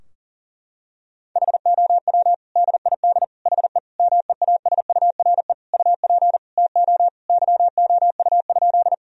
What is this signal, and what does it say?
Morse code, heard as a keyed tone, about 40 wpm, 700 Hz. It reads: HOW DID HE MEASURE UP TO YOU?
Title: Sentences from Top 1000 Words - Review Part 2 - 40wpm